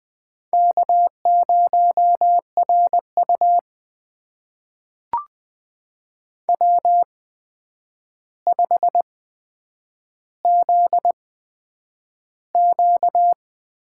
Morse code, heard as a keyed tone, about 20 wpm, spaced as for 9 wpm, 700 Hz. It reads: K0RU W5ZQ
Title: USA Call Signs - 4 Characters - 8x Character Spacing - 20wpm